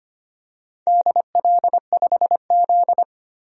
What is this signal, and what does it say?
Morse code, heard as a keyed tone, about 25 wpm, 700 Hz. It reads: DL5Z
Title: DXCC-100 Call Signs - 4 Characters - 25wpm